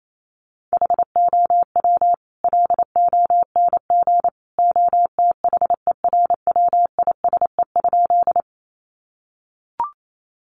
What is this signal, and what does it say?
Morse code, heard as a keyed tone, about 28 wpm, 700 Hz. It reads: HOW LONG OTHERWISE?
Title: Sentences from Top 1200 Words 28wpm